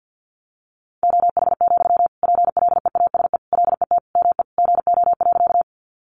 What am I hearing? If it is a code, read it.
Morse code, heard as a keyed tone, about 50 wpm, 700 Hz. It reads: OH, PLEASE LET ME GO.
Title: Sentences from Top 1500 Words 50wpm